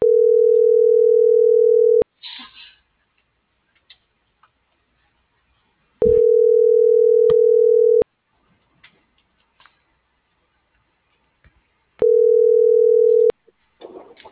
Ringback tone (North America), 440 + 480 Hz.